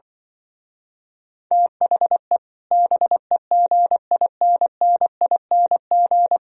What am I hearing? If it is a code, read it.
Morse code, heard as a keyed tone, about 25 wpm, 700 Hz. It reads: THE BEGINNING
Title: Speed-Racing - Sets of 3 Words 20wpm